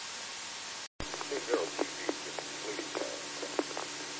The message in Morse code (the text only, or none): none